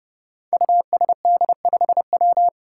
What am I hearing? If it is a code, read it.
Morse code, heard as a keyed tone, about 30 wpm, 700 Hz. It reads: USD5W
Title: Five Challenging Letter-Number Combinations 30wpm